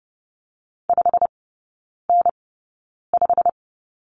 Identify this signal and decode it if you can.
Morse code, heard as a keyed tone, about 30 wpm, spaced as for 9 wpm, 700 Hz. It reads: 5N5
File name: Lesson 18e - Letter m - Three Characters 30wpm